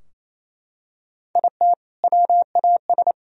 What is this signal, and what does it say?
Morse code, heard as a keyed tone, about 28 wpm, 700 Hz. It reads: IT WAS
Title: Sentences from Top 1000 Words - Review Part 3 - 28wpm